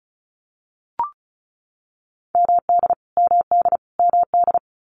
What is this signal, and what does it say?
Morse code, heard as a keyed tone, about 34 wpm, 700 Hz. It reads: MD MD MD
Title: US States Repeated 3x 35wpm